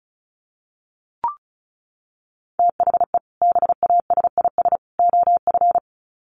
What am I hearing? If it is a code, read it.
Morse code, heard as a keyed tone, about 34 wpm, 700 Hz. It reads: THE BASIS OF
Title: Sets of 3 Words 35wpm